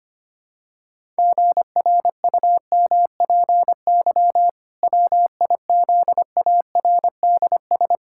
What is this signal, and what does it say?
Morse code, heard as a keyed tone, about 25 wpm, 700 Hz. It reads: GRUMPY WIZARDS
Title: Pangram Challenge v2 - 25wpm